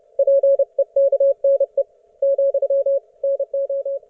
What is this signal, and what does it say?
Morse code, heard as a keyed tone, about 30 wpm, 550 Hz. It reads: PEKNE , NO
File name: Archiv / CUC181 / ok1fhi cul181